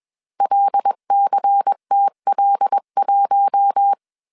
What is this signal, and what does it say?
Morse code, heard as a keyed tone, about 21 wpm, 800 Hz. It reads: LCTL1
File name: CW / DET / 1 6 1500C S C 20wpm 2116